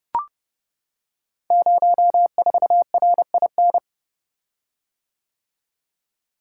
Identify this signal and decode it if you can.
Morse code, heard as a keyed tone, about 30 wpm, 700 Hz. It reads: 04RIN
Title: Code Groups - 5 Letter:Numbers - 30wpm